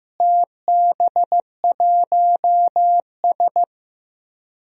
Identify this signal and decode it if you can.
Morse code, heard as a keyed tone, about 15 wpm, 700 Hz. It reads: TB1S
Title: DXCC-100 Call Signs - 4 Characters - Repeated 3x - 15wpm